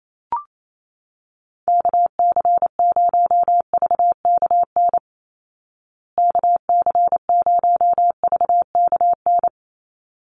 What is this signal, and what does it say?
Morse code, heard as a keyed tone, about 27 wpm, 700 Hz. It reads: KC0VKN KC0VKN
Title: Call Signs Repeated 3x - 28wpm